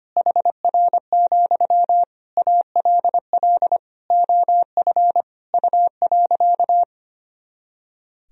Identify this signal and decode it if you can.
Morse code, heard as a keyed tone, about 25 wpm, 700 Hz. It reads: HR, ALL OF U.
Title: Sentences from Top 500 Words and Abbreviations 25wpm